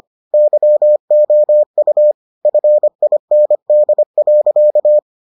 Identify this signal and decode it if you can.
Morse code, heard as a keyed tone, about 25 wpm, 600 Hz. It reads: YOU FIND.